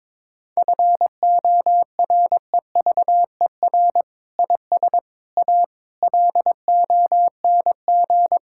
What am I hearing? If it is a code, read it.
Morse code, heard as a keyed tone, about 22 wpm, 700 Hz. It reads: FOREVER IS A LONG
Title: Sentences from Top 1600 Words 22wpm